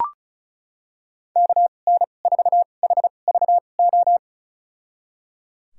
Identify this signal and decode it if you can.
Morse code, heard as a keyed tone, about 36 wpm, spaced as for 25 wpm, 700 Hz. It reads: KN4HVO
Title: USA Call Signs - 6 Characters - 2x Character Spacing - 35wpm